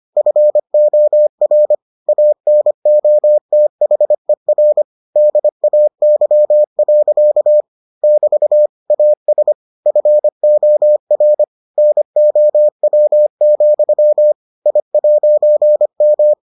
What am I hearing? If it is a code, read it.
Morse code, heard as a keyed tone, about 25 wpm, 600 Hz. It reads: FOR ANOTHER DAY. <BT> AS FOR NOW, I'M